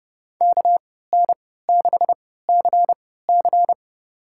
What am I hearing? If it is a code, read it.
Morse code, heard as a keyed tone, about 31 wpm, spaced as for 17 wpm, 700 Hz. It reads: KN6CC
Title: USA Call Signs - 5 Characters - 3x Character Spacing - 30wpm